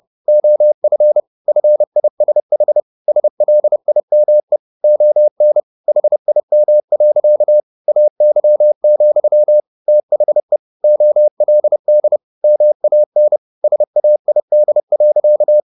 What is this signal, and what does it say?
Morse code, heard as a keyed tone, about 30 wpm, 600 Hz. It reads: OF FISH SLIME ON HIM. AY, THE OLD MAN SAID.